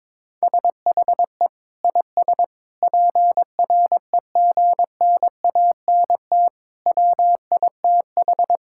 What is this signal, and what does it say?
Morse code, heard as a keyed tone, about 22 wpm, 700 Hz. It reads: SHE IS PREGNANT WITH
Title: Sentences from Top 1000 Words - Review Part 6 - 22wpm